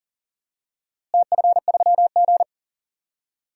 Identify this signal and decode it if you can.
Morse code, heard as a keyed tone, about 40 wpm, 700 Hz. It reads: TF3G
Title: DXCC-100 Call Signs - 4 Characters - Repeated 2x - 40wpm